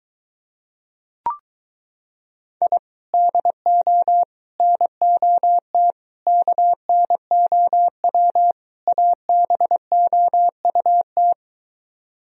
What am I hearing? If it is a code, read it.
Morse code, heard as a keyed tone, about 23 wpm, 700 Hz. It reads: I DO NOT KNOW ABOUT